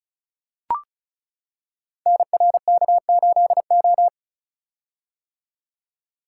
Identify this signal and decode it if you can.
Morse code, heard as a keyed tone, about 36 wpm, 700 Hz. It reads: NRK8O
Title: Code Groups - 5 Letter:Numbers - 35wpm